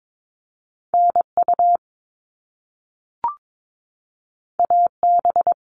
Morse code, heard as a keyed tone, about 22 wpm, 700 Hz. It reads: NU AB
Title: Canadian Provinces and Territories 22wpm